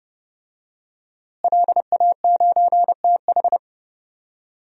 Morse code, heard as a keyed tone, about 30 wpm, 700 Hz. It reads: LA9TH
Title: DXCC-100 Call Signs - 5 Characters - Repeated 3x - 30wpm